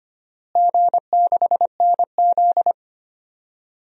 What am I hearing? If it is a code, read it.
Morse code, heard as a keyed tone, about 25 wpm, 700 Hz. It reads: G6NZ
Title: DXCC-100 Call Signs - 4 Characters - Repeated 2x - 25wpm